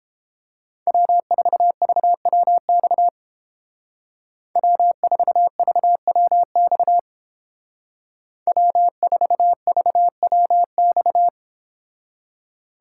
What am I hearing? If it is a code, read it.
Morse code, heard as a keyed tone, about 30 wpm, 700 Hz. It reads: W4VWX W4VWX W4VWX